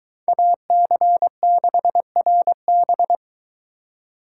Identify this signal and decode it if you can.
Morse code, heard as a keyed tone, about 22 wpm, 700 Hz. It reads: AC6RB